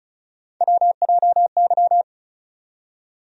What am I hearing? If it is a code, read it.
Morse code, heard as a keyed tone, about 34 wpm, 700 Hz. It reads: WJY
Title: USA Call Sign Postfixes - 3 Letters 35wpm